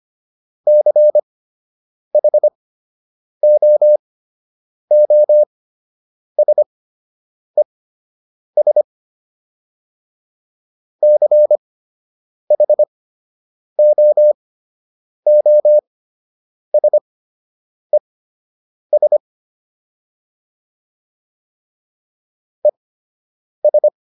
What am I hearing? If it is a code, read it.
Morse code, heard as a keyed tone, about 25 wpm, spaced as for 8 wpm, 600 Hz. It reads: CHOOSES CHOOSES ES